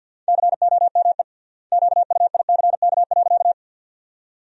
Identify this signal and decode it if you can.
Morse code, heard as a keyed tone, about 48 wpm, 700 Hz. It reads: COME QUICK.
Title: 3x Word-Spacing - Sentences from Top 500 Words - 50wpm